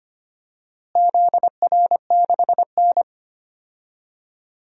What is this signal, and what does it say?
Morse code, heard as a keyed tone, about 25 wpm, 700 Hz. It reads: ZR6N
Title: DXCC-100 Call Signs - 4 Characters - Repeated 3x - 25wpm